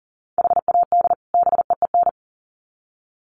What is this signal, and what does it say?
Morse code, heard as a keyed tone, about 40 wpm, 700 Hz. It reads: HAD BEEN
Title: Sets of 2 Words 40wpm